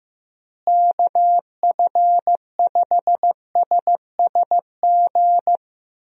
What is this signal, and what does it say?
Morse code, heard as a keyed tone, about 16 wpm, 700 Hz. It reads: KF5SSG
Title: USA Call Signs - 6 Characters - Repeated 3x - 15wpm